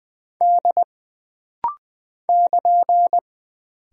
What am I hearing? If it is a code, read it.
Morse code, heard as a keyed tone, about 20 wpm, 700 Hz. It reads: D <KN>
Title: Single Character (Letter:Number:Punctuation:Prosign) - Rapid-Fire - 20wpm